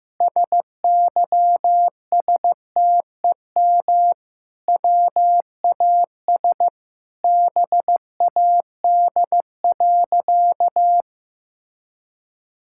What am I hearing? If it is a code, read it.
Morse code, heard as a keyed tone, about 15 wpm, 700 Hz. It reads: SYSTEM WAS BAD.
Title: Sentences from Top 1000 Words - Review Part 5 - 15wpm